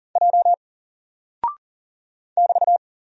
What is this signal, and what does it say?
Morse code, heard as a keyed tone, about 40 wpm, 700 Hz. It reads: J <BT>
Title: Single Character (Letter:Number:Punctuation:Prosign) Alt 40wpm